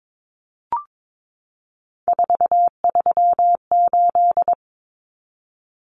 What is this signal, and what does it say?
Morse code, heard as a keyed tone, about 22 wpm, 700 Hz. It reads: 438
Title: Numbers - 3-Digit 22wpm